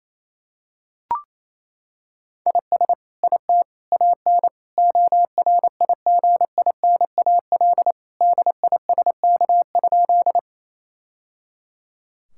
Morse code, heard as a keyed tone, about 28 wpm, 700 Hz. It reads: IS IT AN ORIGINAL DISK?
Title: Sentences from Top 1000 Words - Review Part 6 - 28wpm